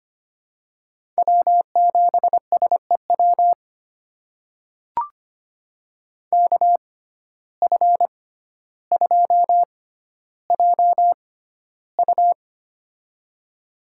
Morse code, heard as a keyed tone, about 25 wpm, spaced as for 13 wpm, 700 Hz. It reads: W7SEW KF2JU
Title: USA Call Signs - 5 Characters - 6x Character Spacing - 25wpm